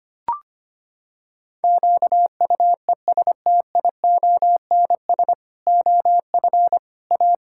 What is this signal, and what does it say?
Morse code, heard as a keyed tone, about 25 wpm, 700 Hz. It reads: QUESTIONS OF A
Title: Pangram Challenge v2 - 25wpm